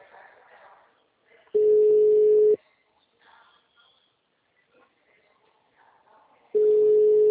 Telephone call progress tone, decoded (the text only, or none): ringback tone